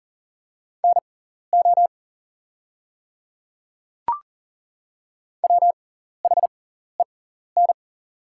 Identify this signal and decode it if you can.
Morse code, heard as a keyed tone, about 39 wpm, spaced as for 14 wpm, 700 Hz. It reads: NO WHEN
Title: 6x Character Spacing - Top 100 Words - 40wpm